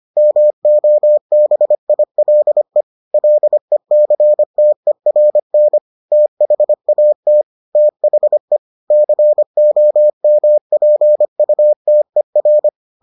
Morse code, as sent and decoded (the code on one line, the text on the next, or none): -- --- -... .. .-.. / .-.. -.-. - . .-. -. / - .... .- - / - .... / -.-. --- -- .--. ..- - . .-.
MOBILE LECTERN THAT THE COMPUTER